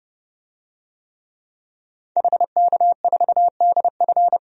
Morse code, heard as a keyed tone, about 32 wpm, 700 Hz. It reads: HK4DF
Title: DXCC-100 Call Signs - 5 Characters - Repeated 2x - 30wpm